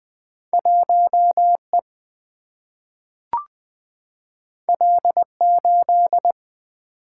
Morse code, heard as a keyed tone, about 21 wpm, 700 Hz. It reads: 1E L8
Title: Single-Letter Single-Number 20wpm